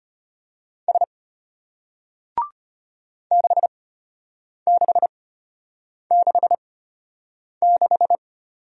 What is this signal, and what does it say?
Morse code, heard as a keyed tone, about 33 wpm, spaced as for 8 wpm, 700 Hz. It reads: S 6666